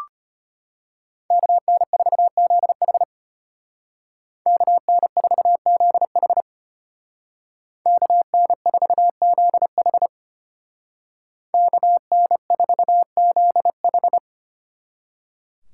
Morse code, heard as a keyed tone, about 31 wpm, 700 Hz. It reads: KN4ZH KN4ZH KN4ZH KN4ZH
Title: Speed Racing - USA 2x2 Call Signs 25wpm